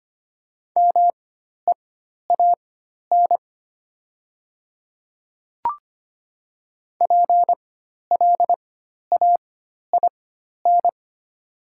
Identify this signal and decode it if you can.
Morse code, heard as a keyed tone, about 25 wpm, spaced as for 12 wpm, 700 Hz. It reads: MEAN PLAIN
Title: 4x Character Spacing - Top 500 Words - 25wpm